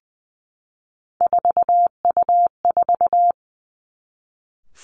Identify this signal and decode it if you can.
Morse code, heard as a keyed tone, about 20 wpm, 700 Hz. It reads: 4U4
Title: Three Challenging Letter-Number Combinations 20wpm